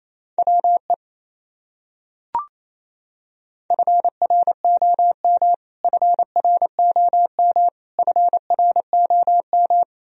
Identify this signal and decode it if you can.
Morse code, heard as a keyed tone, about 28 wpm, 700 Hz. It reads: WE FROM FROM FROM
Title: Top 100 Words Repeated 28wpm